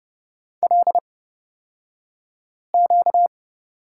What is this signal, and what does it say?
Morse code, heard as a keyed tone, about 30 wpm, spaced as for 5 wpm, 700 Hz. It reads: LQ
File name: Lesson 38d - Letter q - Two Characters 30wpm